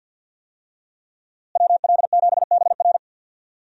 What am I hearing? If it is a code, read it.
Morse code, heard as a keyed tone, about 52 wpm, 700 Hz. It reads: WL7BR
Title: USA Call Signs - 5 Characters - Repeated 3x - 50wpm